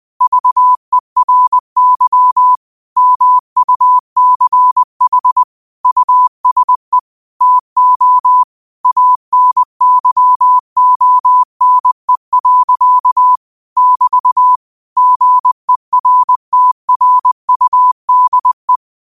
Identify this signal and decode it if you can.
Morse code, heard as a keyed tone, about 20 wpm, 1000 Hz. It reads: VERY MUCH USE TO ANYONE. <BT> GERTRUDE